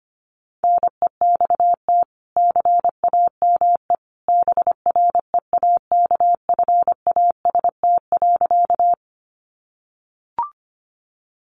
Morse code, heard as a keyed tone, about 25 wpm, 700 Hz. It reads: NEXT CAME BREAKFAST.
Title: Sentences from Top 1500 Words 25wpm